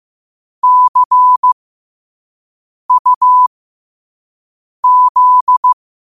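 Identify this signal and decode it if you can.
Morse code, heard as a keyed tone, about 15 wpm, spaced as for 5 wpm, 1000 Hz. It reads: CUZ
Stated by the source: Morse code practice at 5 wpm.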